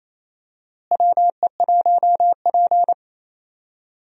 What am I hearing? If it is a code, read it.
Morse code, heard as a keyed tone, about 27 wpm, 700 Hz. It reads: WE1P
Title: USA Call Signs - 4 Characters - Repeated 3x - 28wpm